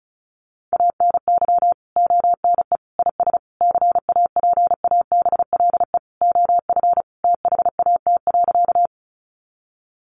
Morse code, heard as a keyed tone, about 34 wpm, 700 Hz. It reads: ANY ONE IS CAPABLE OF THAT.
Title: Sentences from Top 1200 Words 35wpm